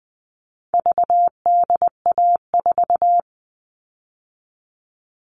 Morse code, heard as a keyed tone, about 20 wpm, 700 Hz. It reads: VDA4
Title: All Cut Numbers - 4-Digit 20wpm